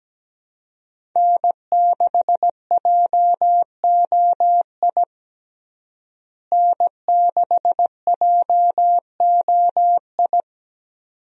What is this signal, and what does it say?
Morse code, heard as a keyed tone, about 17 wpm, 700 Hz. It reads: N6JOI N6JOI
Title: USA Call Signs - 5 Characters - Repeated 2x - 17wpm